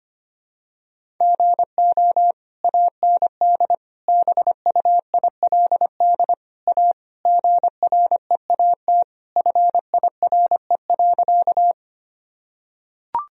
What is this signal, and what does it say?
Morse code, heard as a keyed tone, about 25 wpm, 700 Hz. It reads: GO AND BUILD A GREAT FIRE.
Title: Sentences from Top 1000 Words - Review Part 1 - 25wpm